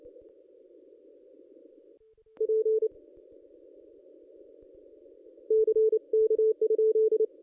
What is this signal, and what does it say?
Morse code, heard as a keyed tone, about 28 wpm, 440 Hz. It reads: P C K?
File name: contests / ss-cw-k5tr-op-n5rz -1400